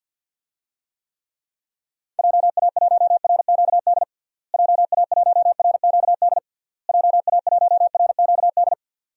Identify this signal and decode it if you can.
Morse code, heard as a keyed tone, about 51 wpm, 700 Hz. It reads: JA1RQD JA1RQD JA1RQD
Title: DXCC-100 Call Signs - 6 Characters - Repeated 3x - 50wpm